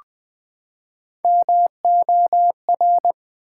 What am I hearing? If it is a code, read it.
Morse code, heard as a keyed tone, about 20 wpm, 700 Hz. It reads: MOR
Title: Build A Word - 3 Letters At A Time - 8-Letter Words - 20wpm